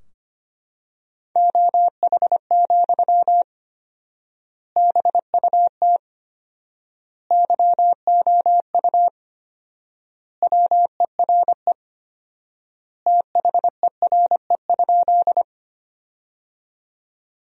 Morse code, heard as a keyed tone, about 25 wpm, 700 Hz. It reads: OH, BUT YOU WERE THERE?
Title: 4x Word-Spacing - Sentences from Top 500 Words - 25wpm